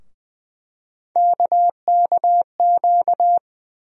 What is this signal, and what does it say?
Morse code, heard as a keyed tone, about 20 wpm, 700 Hz. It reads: KKQ
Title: USA Call Sign Postfixes - 3 Letters 20wpm